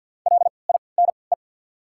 Morse code, heard as a keyed tone, about 48 wpm, spaced as for 28 wpm, 700 Hz. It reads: LINE
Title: 3x Character Spacing - Top 300 Words - 50wpm